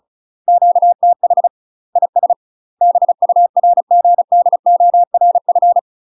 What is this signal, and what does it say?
Morse code, heard as a keyed tone, about 34 wpm, 700 Hz. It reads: QTH IS BURGDORF
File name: UK MASTER QSOs TRAINING / 35WPM / UK MASTER QSOs 0006